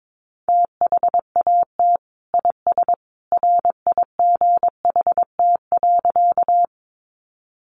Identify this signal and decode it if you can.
Morse code, heard as a keyed tone, about 22 wpm, 700 Hz. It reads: THAT IS RIGHT.
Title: Sentences from Top 200 Words 22wpm